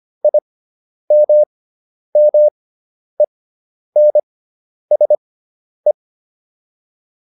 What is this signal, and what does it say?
Morse code, heard as a keyed tone, about 25 wpm, spaced as for 10 wpm, 600 Hz. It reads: IMMENSE